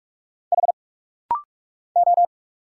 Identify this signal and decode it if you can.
Morse code, heard as a keyed tone, about 44 wpm, 700 Hz. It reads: H O
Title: Single Letters - Rapid-Fire 45wpm